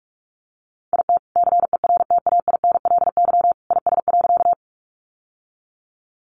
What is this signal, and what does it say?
Morse code, heard as a keyed tone, about 44 wpm, 700 Hz. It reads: IT CERTAINLY IS.
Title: Sentences from Top 900 Words 45wpm